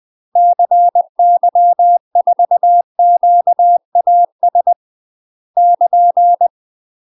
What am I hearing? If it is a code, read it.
Morse code, heard as a keyed tone, about 20 wpm, 700 Hz. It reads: CY4QAS <KN>
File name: UK MASTER QSOs TRAINING / 20WPM / UK MASTER QSOs 0069